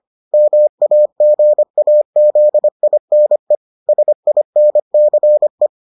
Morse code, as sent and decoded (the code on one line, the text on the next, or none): -- .- --. .- --.. .. -. / ... .. -. -.-. .
MAGAZINE SINCE